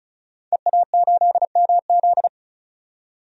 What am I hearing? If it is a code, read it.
Morse code, heard as a keyed tone, about 34 wpm, 700 Hz. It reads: EA8MZ
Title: DXCC-100 Call Signs - 5 Characters - 35wpm